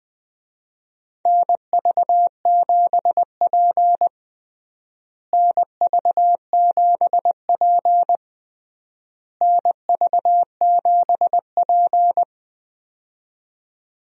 Morse code, heard as a keyed tone, about 21 wpm, 700 Hz. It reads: NV7P NV7P NV7P
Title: USA Call Signs - 4 Characters - Repeated 3x - 20wpm